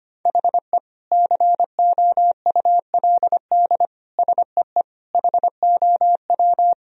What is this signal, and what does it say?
Morse code, heard as a keyed tone, about 25 wpm, 700 Hz. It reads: HE COULD SEE HOW
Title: Sentences from Top 1000 Words - Review Part 3 - 25wpm